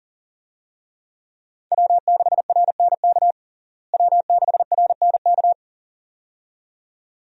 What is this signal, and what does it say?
Morse code, heard as a keyed tone, about 40 wpm, 700 Hz. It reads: W6RNK W6RNK
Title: USA Call Signs - 5 Characters - Repeated 2x - 40wpm